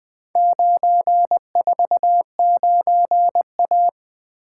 Morse code, heard as a keyed tone, about 20 wpm, 700 Hz. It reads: 949A
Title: Four Letter-Number Combinations - 20wpm